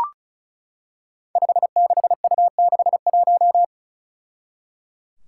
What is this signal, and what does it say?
Morse code, heard as a keyed tone, about 34 wpm, 700 Hz. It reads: 56U61
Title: Five Challenging Letter-Number Combinations 35wpm